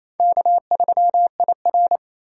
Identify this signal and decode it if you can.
Morse code, heard as a keyed tone, about 35 wpm, 700 Hz. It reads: K3IR
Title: Speed Racing - USA 1x2 Call Signs 28wpm